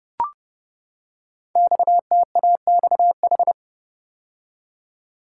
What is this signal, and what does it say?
Morse code, heard as a keyed tone, about 28 wpm, 700 Hz. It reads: XTAXH
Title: Code Groups - 5 Letters - 30wpm